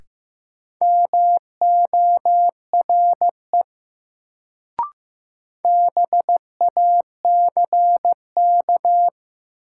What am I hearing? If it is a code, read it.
Morse code, heard as a keyed tone, about 15 wpm, 700 Hz. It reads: MORE BACK
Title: Sets of 2 Words - New Word Drill - 15wpm